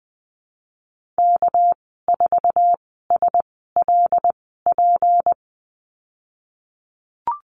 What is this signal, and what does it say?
Morse code, heard as a keyed tone, about 20 wpm, spaced as for 15 wpm, 700 Hz. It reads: K4SLP
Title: 200 Lesson 32g - Letter k - Call Signs 20wpm